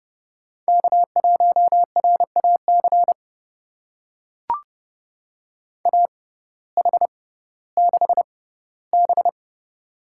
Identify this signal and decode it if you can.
Morse code, heard as a keyed tone, about 30 wpm, spaced as for 17 wpm, 700 Hz. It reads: K1RAC AH6B
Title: USA Call Signs - 6x Character Spacing - 30wpm